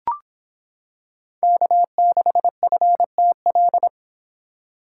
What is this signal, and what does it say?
Morse code, heard as a keyed tone, about 26 wpm, 700 Hz. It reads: K6FTL